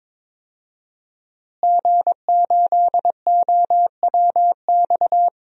Morse code, heard as a keyed tone, about 22 wpm, 700 Hz. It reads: G8OWX
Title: DXCC-100 Call Signs - 5 Characters - Repeated 3x - 22wpm